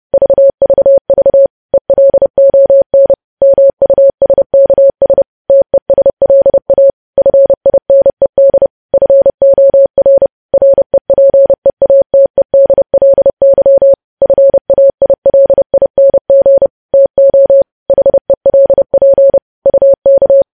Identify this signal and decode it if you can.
Morse code, heard as a keyed tone, about 30 wpm, 550 Hz. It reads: VVV ELON MUSKS TESLA FINED FOR REPEATEDLY FAILING TO HELP UK